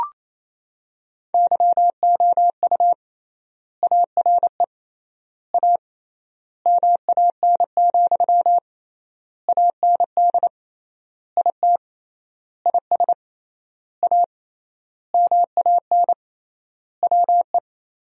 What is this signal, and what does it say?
Morse code, heard as a keyed tone, about 28 wpm, 700 Hz. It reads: YOU ARE A MAN, AND IT IS A MAN WE
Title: 3x Word-Spacing - Sentences from Top 200 Words - 28wpm